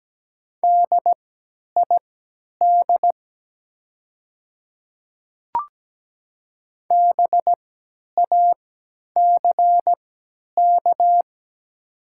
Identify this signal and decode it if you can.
Morse code, heard as a keyed tone, about 17 wpm, spaced as for 10 wpm, 690 Hz. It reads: DID BACK
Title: 3x Character Spacing - Top 200 Words - 17wpm